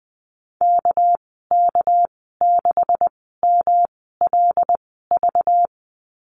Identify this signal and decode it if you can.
Morse code, heard as a keyed tone, about 20 wpm, spaced as for 15 wpm, 700 Hz. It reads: KK6MLV
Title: 213 Lesson 34g - Number 8 - Call Signs 20wpm